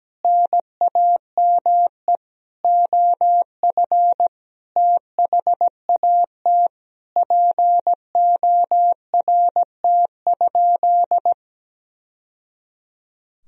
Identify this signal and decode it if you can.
Morse code, heard as a keyed tone, about 17 wpm, 700 Hz. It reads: NAME OF THAT PORT?
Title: Sentences from Top 1000 Words - Review Part 1 - 17wpm